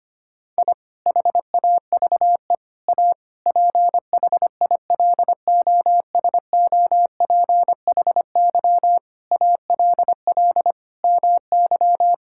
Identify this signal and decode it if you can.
Morse code, heard as a keyed tone, about 25 wpm, 700 Hz. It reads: I HAVE A PHILOSOPHY ALL MY
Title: Sentences from Top 1000 Words - Review Part 6 - 25wpm